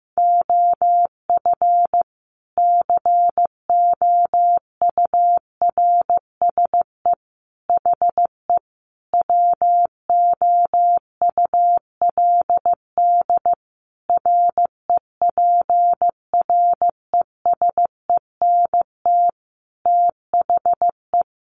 Morse code, heard as a keyed tone, about 15 wpm, 700 Hz. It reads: OF COURSE HE WOULD REPRESENT THE
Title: Sentences from Top 1200 Words 15wpm